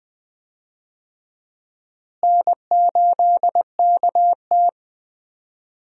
Morse code, heard as a keyed tone, about 20 wpm, 700 Hz. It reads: N8KT
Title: USA Call Signs - 4 Characters - Repeated 2x - 20wpm